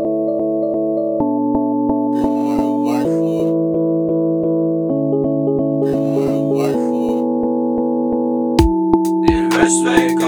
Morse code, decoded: none